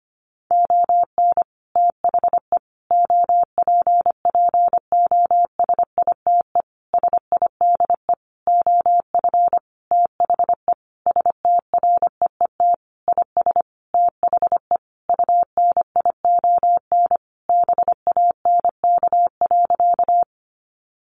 Morse code, heard as a keyed tone, about 25 wpm, 700 Hz. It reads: ON THE OPPOSITE SIDE OF THE STREET IS THE UNION BANK.
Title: Sentences from Top 1400 Words 25wpm